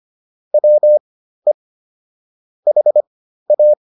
Morse code, heard as a keyed tone, about 25 wpm, spaced as for 13 wpm, 600 Hz. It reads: WE HA
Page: intermediate-practice-files